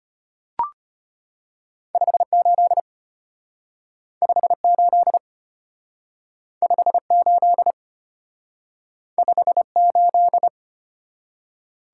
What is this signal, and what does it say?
Morse code, heard as a keyed tone, about 33 wpm, 700 Hz. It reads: 58 58 58 58